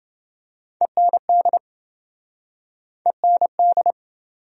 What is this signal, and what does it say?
Morse code, heard as a keyed tone, about 29 wpm, 700 Hz. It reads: END END